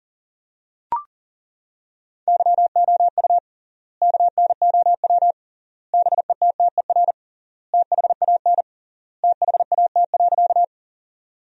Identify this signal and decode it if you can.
Morse code, heard as a keyed tone, about 40 wpm, 700 Hz. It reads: YOU KNOW BETTER THAN THAT.
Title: 3x Word-Spacing - Sentences from Top 400 Words - 40wpm